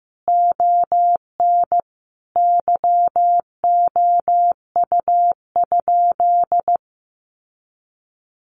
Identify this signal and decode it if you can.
Morse code, heard as a keyed tone, about 14 wpm, 700 Hz. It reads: ON YOU?
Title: Sentences from Top 1200 Words 15wpm